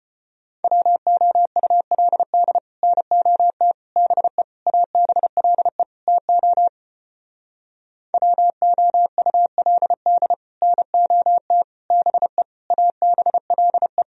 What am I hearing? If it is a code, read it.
Morse code, heard as a keyed tone, about 33 wpm, 700 Hz. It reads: WOULD NOT BE ABLE TO WOULD NOT BE ABLE